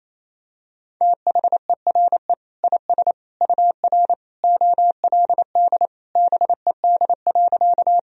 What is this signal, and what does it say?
Morse code, heard as a keyed tone, about 28 wpm, 700 Hz. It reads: THERE IS UR OLD BED.
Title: Sentences from Top 500 Words and Abbreviations 28wpm